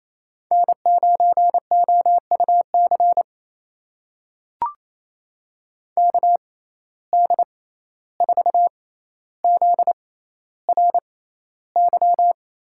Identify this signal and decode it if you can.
Morse code, heard as a keyed tone, about 28 wpm, spaced as for 14 wpm, 700 Hz. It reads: N9OUC KD4ZRY
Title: USA Call Signs - 6x Character Spacing - 28wpm